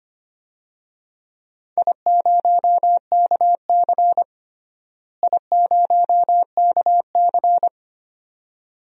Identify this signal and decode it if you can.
Morse code, heard as a keyed tone, about 25 wpm, 700 Hz. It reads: I0KC I0KC
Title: DXCC-100 Call Signs - 4 Characters - Repeated 2x - 25wpm